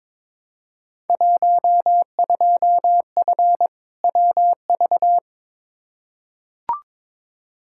Morse code, heard as a keyed tone, about 22 wpm, 700 Hz. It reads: 12F WV
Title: Field Day Exchanges Encore - 22wpm